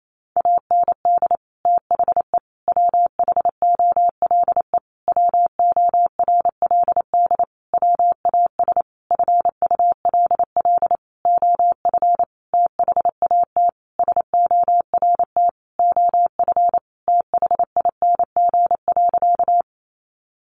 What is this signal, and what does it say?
Morse code, heard as a keyed tone, about 28 wpm, 700 Hz. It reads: AND THE WHOLE WORLD WAS FULL OF THAT SORT OF THING.
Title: Sentences from Top 1300 Words 28wpm